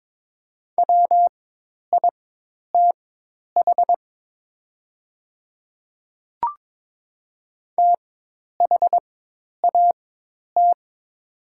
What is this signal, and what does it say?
Morse code, heard as a keyed tone, about 22 wpm, spaced as for 10 wpm, 700 Hz. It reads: WITH THAT
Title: 4x Character Spacing - Top 100 Words - 22wpm